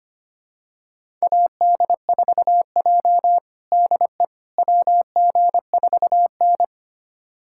Morse code, heard as a keyed tone, about 25 wpm, 700 Hz. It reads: AD4J DE WG4N
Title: USA Callsign DE Callsign - 4 Characters - 25wpm